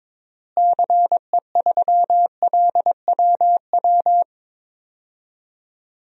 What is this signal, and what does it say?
Morse code, heard as a keyed tone, about 22 wpm, 700 Hz. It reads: CE3LWW
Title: DXCC-100 Call Signs - 6 Characters - Repeated 3x - 22wpm